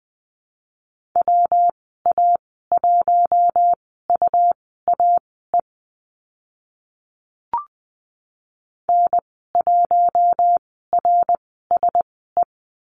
Morse code, heard as a keyed tone, about 20 wpm, spaced as for 14 wpm, 700 Hz. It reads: WA1UAE N1RSE
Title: 122 Lesson 20g - Number 3 - Call Signs 20wpm